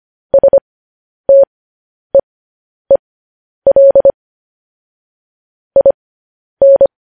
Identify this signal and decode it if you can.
Morse code, heard as a keyed tone, about 25 wpm, spaced as for 10 wpm, 550 Hz. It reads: STEEL IN
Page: news headlines in morse code at 10 wpm